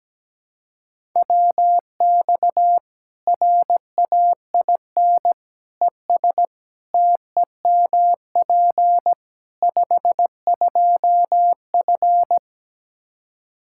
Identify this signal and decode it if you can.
Morse code, heard as a keyed tone, about 17 wpm, 700 Hz. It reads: WX RAIN ES TEMP 52F
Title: Practice QSO - Exchange 2 - Info - 17wpm